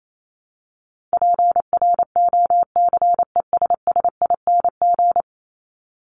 Morse code, heard as a keyed tone, about 28 wpm, 700 Hz. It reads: PROCESSING
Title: ING Suffix - Top 100 Words with Base Words 28wpm